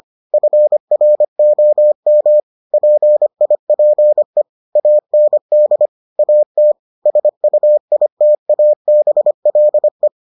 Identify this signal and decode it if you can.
Morse code, heard as a keyed tone, about 25 wpm, 600 Hz. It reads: FROM PIPE AND AT SUITABLE